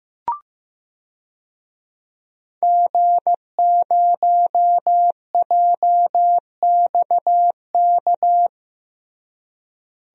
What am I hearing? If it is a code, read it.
Morse code, heard as a keyed tone, about 15 wpm, 700 Hz. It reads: G0JXK
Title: DXCC-100 Call Signs - 5 Characters - Repeated 2x - 15wpm